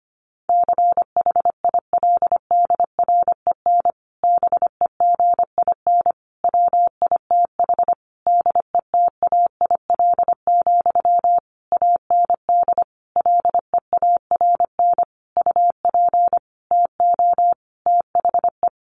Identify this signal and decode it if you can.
Morse code, heard as a keyed tone, about 25 wpm, 700 Hz. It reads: CHILDREN BEGIN WITH DETAIL, AND LEARN UP TO THE
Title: Sentences from Top 1500 Words 25wpm